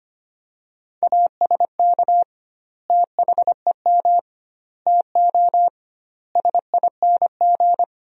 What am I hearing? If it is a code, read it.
Morse code, heard as a keyed tone, about 25 wpm, 700 Hz. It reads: ASK THEM TO SING
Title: 2x Word-Spacing - Sentences from Top 400 Words - 25wpm